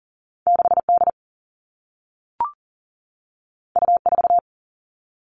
Morse code, heard as a keyed tone, about 40 wpm, 700 Hz. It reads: 6D U4
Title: All Cut Numbers - 2-Digit 40wpm